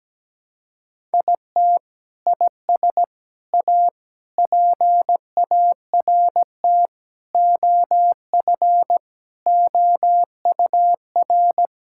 Morse code, heard as a keyed tone, about 17 wpm, 700 Hz. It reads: IT IS A PART OF OUR
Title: Sentences from Top 1000 Words - Review Part 5 - 17wpm